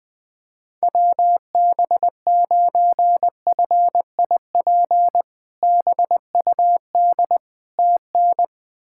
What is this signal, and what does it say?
Morse code, heard as a keyed tone, about 20 wpm, 700 Hz. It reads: WB9FIP BUD TN
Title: K1USN SST QSO Practice - 20wpm